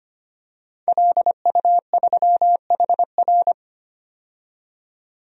Morse code, heard as a keyed tone, about 25 wpm, 700 Hz. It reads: LU3HR
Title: DXCC-100 Call Signs - 5 Characters - 25wpm